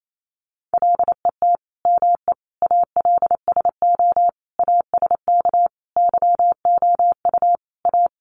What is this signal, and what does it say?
Morse code, heard as a keyed tone, about 28 wpm, 700 Hz. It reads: LET ME ALSO ASK YOU A
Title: Sentences from Top 400 Words 28wpm